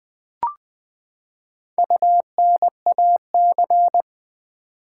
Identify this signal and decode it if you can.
Morse code, heard as a keyed tone, about 20 wpm, 700 Hz. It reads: UNAC
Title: Build A Word - 4 Letters At A Time - 12-Letter Words - 20wpm